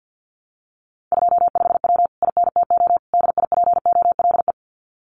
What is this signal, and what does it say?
Morse code, heard as a keyed tone, about 50 wpm, 700 Hz. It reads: 25W INTO DIPOLE
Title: Common QSO Phrases Encore 50wpm